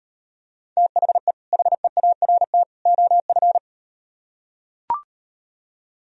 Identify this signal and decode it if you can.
Morse code, heard as a keyed tone, about 38 wpm, 700 Hz. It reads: THE HEART OF